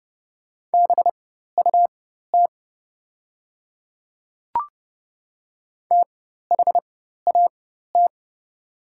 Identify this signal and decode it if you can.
Morse code, heard as a keyed tone, about 30 wpm, spaced as for 14 wpm, 700 Hz. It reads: BUT THAT
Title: 4x Character Spacing - Top 100 Words - 30wpm